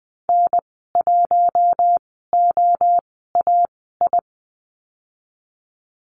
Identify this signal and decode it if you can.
Morse code, heard as a keyed tone, about 20 wpm, spaced as for 14 wpm, 700 Hz. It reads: N1OAI
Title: 042 Lesson 8g - Number 1 - Call Signs 20wpm